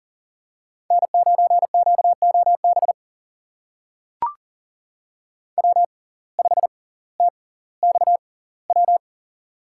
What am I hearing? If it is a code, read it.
Morse code, heard as a keyed tone, about 40 wpm, spaced as for 20 wpm, 700 Hz. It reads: N9QOB W5TXW
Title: USA Call Signs - 5 Characters - 6x Character Spacing - 40wpm